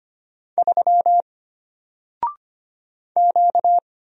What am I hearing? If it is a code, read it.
Morse code, heard as a keyed tone, about 25 wpm, 700 Hz. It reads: 3 Q
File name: Single-Letter-Number-25wpm